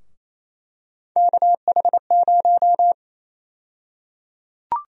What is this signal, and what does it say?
Morse code, heard as a keyed tone, about 28 wpm, 700 Hz. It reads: KH0
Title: USA Call Sign Prefixes - Repeated 2x - 28wpm